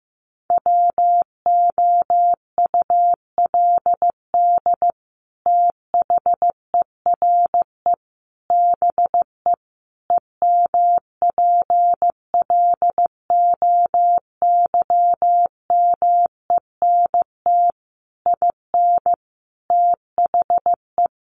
Morse code, heard as a keyed tone, about 15 wpm, 700 Hz. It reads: WOULD THERE BE EMPLOYMENT IN THE
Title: Sentences from Top 1500 Words 15wpm